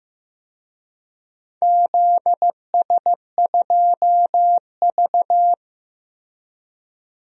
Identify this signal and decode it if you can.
Morse code, heard as a keyed tone, about 15 wpm, 700 Hz. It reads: ZS2V
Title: DXCC-100 Call Signs - 4 Characters - 15wpm